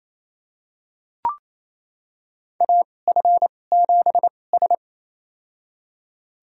Morse code, heard as a keyed tone, about 28 wpm, spaced as for 20 wpm, 700 Hz. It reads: AF7S
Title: USA Call Signs - 4 Characters - 2x Character Spacing - 28wpm